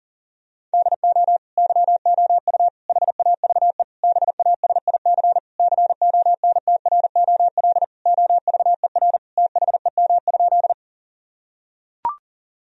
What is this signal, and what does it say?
Morse code, heard as a keyed tone, about 40 wpm, 700 Hz. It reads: DO YOU HAVE BASIC CONTROL OVER THEM?
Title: Sentences from Top 1000 Words - Review Part 3 - 40wpm